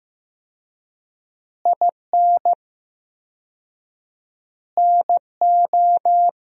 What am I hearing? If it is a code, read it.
Morse code, heard as a keyed tone, about 15 wpm, 700 Hz. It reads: IN NO
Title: 4x Word-Spacing - Sentences from Top 400 Words - 15wpm